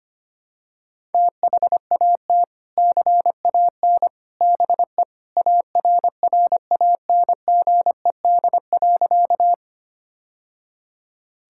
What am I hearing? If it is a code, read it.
Morse code, heard as a keyed tone, about 25 wpm, 700 Hz. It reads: THAT CAN BE ARRANGED.
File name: practice / 25wpm / Sentences with 3-Word Phrases - 25wpm